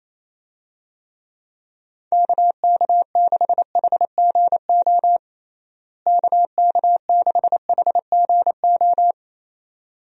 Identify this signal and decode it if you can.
Morse code, heard as a keyed tone, about 28 wpm, 700 Hz. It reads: KK6HGO KK6HGO
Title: USA Call Signs - 6 Characters - Repeated 2x - 28wpm